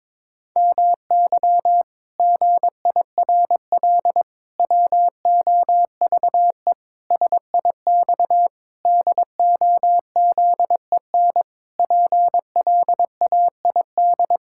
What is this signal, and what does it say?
Morse code, heard as a keyed tone, about 22 wpm, 700 Hz. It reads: MY GIRL WOVE SIX DOZEN PLAID
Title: Pangram Challenge v2 - 22wpm